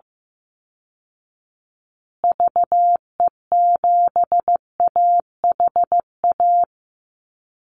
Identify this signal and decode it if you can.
Morse code, heard as a keyed tone, about 15 wpm, 700 Hz. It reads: VE7AHA
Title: Call Signs Repeated 2x - 15wpm